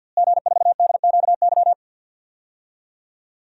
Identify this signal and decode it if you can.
Morse code, heard as a keyed tone, about 52 wpm, 700 Hz. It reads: G4DQY
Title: DXCC-100 Call Signs - 5 Characters - Repeated 2x - 50wpm